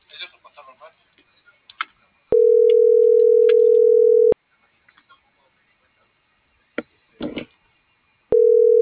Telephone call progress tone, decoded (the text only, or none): ringback tone